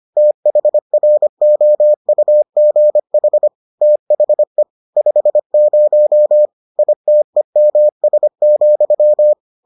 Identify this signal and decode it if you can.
Morse code, heard as a keyed tone, about 25 wpm, 600 Hz. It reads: THROUGH THE 50 ITEMS,